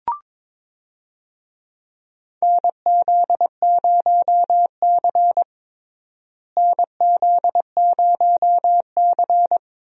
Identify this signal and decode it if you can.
Morse code, heard as a keyed tone, about 22 wpm, 700 Hz. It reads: NZ0C NZ0C
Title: USA Call Signs - 4 Characters - Repeated 2x - 22wpm